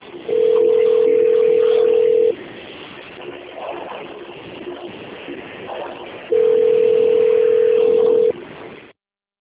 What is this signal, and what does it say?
Ringback tone (North America), 440 + 480 Hz.